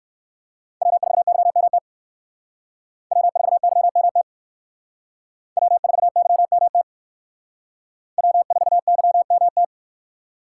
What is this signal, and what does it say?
Morse code, heard as a keyed tone, about 60 wpm, 700 Hz. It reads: W4YMT W4YMT W4YMT W4YMT